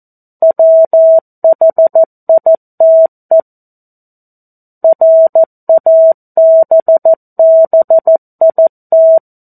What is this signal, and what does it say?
Morse code, heard as a keyed tone, about 14 wpm, 650 Hz. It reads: WHITE RABBIT
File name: 02-Chapter-02 14 WPM